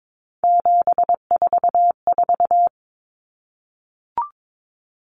Morse code, heard as a keyed tone, about 22 wpm, 700 Hz. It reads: 744
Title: Numbers - 3-Digit 22wpm